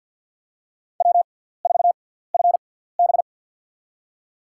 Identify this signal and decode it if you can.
Morse code, heard as a keyed tone, about 51 wpm, spaced as for 17 wpm, 700 Hz. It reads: W4FB
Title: USA Call Signs - 4 Characters - 6x Character Spacing - 50wpm